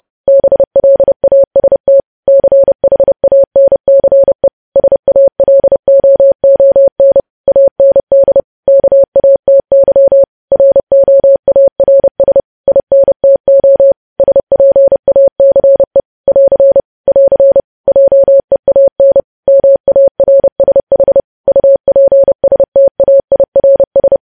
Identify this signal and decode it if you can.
Morse code, heard as a keyed tone, about 30 wpm, 550 Hz. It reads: BLAST CHANCE SALOON AND KATY ROARS INTO SPACE <AR> <AR> JEAN MARSH UPSTAIRS
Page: news headlines in morse code at 30 wpm